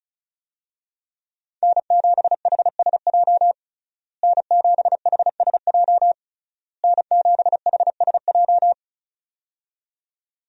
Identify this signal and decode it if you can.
Morse code, heard as a keyed tone, about 36 wpm, 700 Hz. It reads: N7HSJ N7HSJ N7HSJ
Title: USA Call Signs - 5 Characters - Repeated 3x - 35wpm